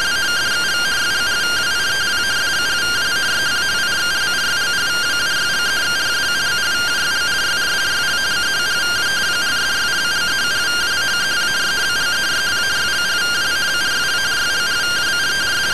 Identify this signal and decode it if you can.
Two-tone teleprinter signal, about 170 Hz shift.